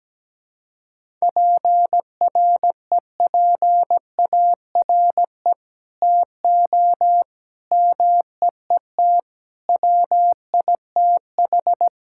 Morse code, heard as a keyed tone, about 17 wpm, 700 Hz. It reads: PREPARE TO MEET WITH
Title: Sentences from Top 1000 Words - Review Part 6 - 17wpm